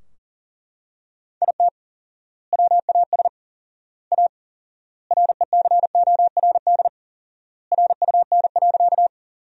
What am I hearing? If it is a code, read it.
Morse code, heard as a keyed tone, about 40 wpm, 700 Hz. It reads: IT WAS A RECORD RUN.
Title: 4x Word-Spacing - Sentences from Top 500 Words - 40wpm